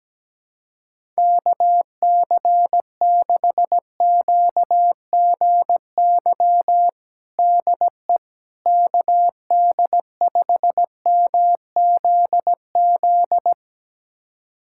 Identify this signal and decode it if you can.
Morse code, heard as a keyed tone, about 17 wpm, 700 Hz. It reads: KC6QGY DE KD5MZZ
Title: USA Callsign DE Callsign - 6 Characters - 17wpm